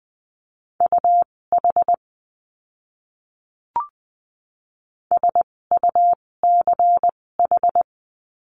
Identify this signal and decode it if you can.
Morse code, heard as a keyed tone, about 20 wpm, spaced as for 16 wpm, 700 Hz. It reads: UH SUCH
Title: 256 Lesson 42a - Words 12-20wpm